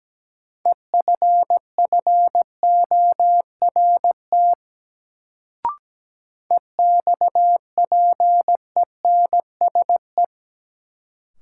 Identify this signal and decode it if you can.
Morse code, heard as a keyed tone, about 17 wpm, 700 Hz. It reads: EFFORT EXPENSE
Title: Sets of 4 Words - New Word Drill - 17wpm